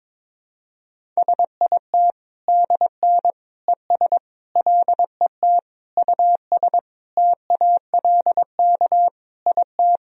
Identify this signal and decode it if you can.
Morse code, heard as a keyed tone, about 22 wpm, 700 Hz. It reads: SIT DN ES LET US TALK IT
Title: Sentences from Top 500 Words and Abbreviations 22wpm